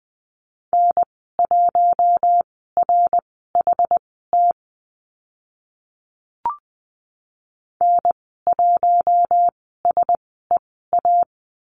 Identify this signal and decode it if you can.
Morse code, heard as a keyed tone, about 20 wpm, spaced as for 15 wpm, 700 Hz. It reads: N1RHT N1SEA
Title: 069 Lesson 12g - Letter d - Call Signs 20wpm